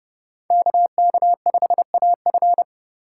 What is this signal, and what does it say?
Morse code, heard as a keyed tone, about 29 wpm, 700 Hz. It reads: KK5AF